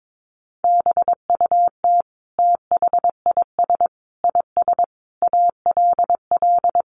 Morse code, heard as a keyed tone, about 22 wpm, 700 Hz. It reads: BUT THIS IS ALL
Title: Sentences from Top 1500 Words 22wpm